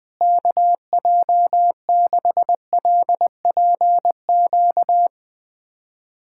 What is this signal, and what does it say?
Morse code, heard as a keyed tone, about 20 wpm, 700 Hz. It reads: KJ6LPQ
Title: USA Callsign DE Callsign - 6 Characters - 20wpm